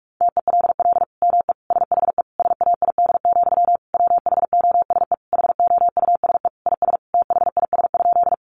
Morse code, heard as a keyed tone, about 44 wpm, 700 Hz. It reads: TELL ME SHE SAID, WHOSE HOUSE IS THIS?
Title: Sentences from Top 1200 Words 45wpm